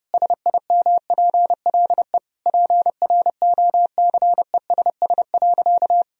Morse code, heard as a keyed tone, about 32 wpm, 700 Hz. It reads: SIMPLE PROCESS.
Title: Sentences from Top 1000 Words - Review Part 3 - 30wpm